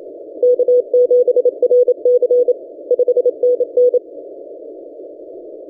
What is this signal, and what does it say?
Morse code, heard as a keyed tone, about 27 wpm, 490 Hz. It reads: K7RC 5NN
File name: ft5zm-17cw-2014-01-27-0120z-lp